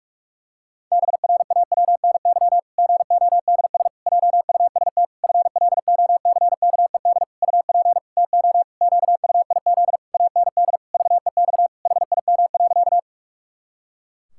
Morse code, heard as a keyed tone, about 48 wpm, 700 Hz. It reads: BRAWNY GODS JUST FLOCKED UP TO QUIZ AND VEX HIM.
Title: Pangram Challenge v2 - 45wpm